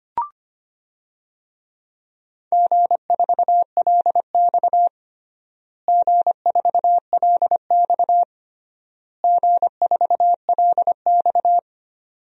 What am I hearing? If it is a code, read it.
Morse code, heard as a keyed tone, about 25 wpm, 700 Hz. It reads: G4LX G4LX G4LX
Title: DXCC-100 Call Signs - 4 Characters - Repeated 3x - 25wpm